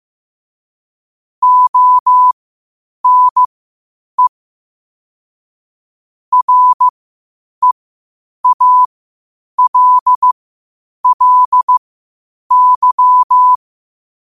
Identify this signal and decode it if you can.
Morse code, heard as a keyed tone, about 15 wpm, spaced as for 8 wpm, 1000 Hz. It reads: ONE REALLY
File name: qotdfiles / 8wpm